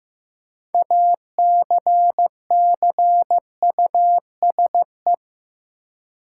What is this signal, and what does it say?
Morse code, heard as a keyed tone, about 15 wpm, 700 Hz. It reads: ACCUSE
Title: Six Letter Words - 15wpm